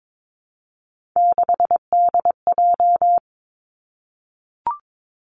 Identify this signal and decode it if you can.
Morse code, heard as a keyed tone, about 22 wpm, 700 Hz. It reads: 6DJ
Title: Three Challenging Letter-Number Combinations 22wpm